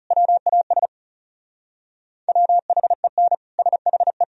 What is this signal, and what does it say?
Morse code, heard as a keyed tone, about 37 wpm, 700 Hz. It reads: WAS WHEN SHE
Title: Speed-Racing - Sets of 3 Words 30wpm